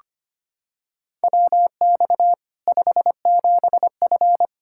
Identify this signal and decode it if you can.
Morse code, heard as a keyed tone, about 25 wpm, 700 Hz. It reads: WX 57F
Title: K1USN SST QSO Practice 25wpm